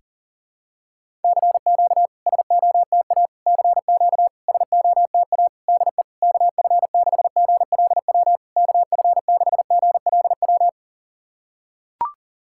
Morse code, heard as a keyed tone, about 37 wpm, 690 Hz. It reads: CQ SOTA CQ SOTA DE KF6GLW KF6GLW
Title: SOTA QSO Practice 40wpm